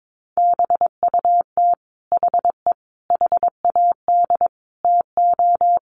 Morse code, heard as a keyed tone, about 22 wpm, 700 Hz. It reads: BUT HE HAD TO
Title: Sentences from Top 800 Words 22wpm